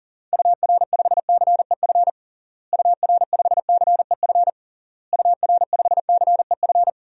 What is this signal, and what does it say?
Morse code, heard as a keyed tone, about 40 wpm, 700 Hz. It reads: UR5CEF UR5CEF UR5CEF
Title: DXCC-100 Call Signs - 6 Characters - Repeated 3x - 40wpm